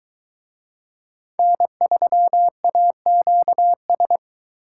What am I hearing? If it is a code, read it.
Morse code, heard as a keyed tone, about 22 wpm, 700 Hz. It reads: N3AQS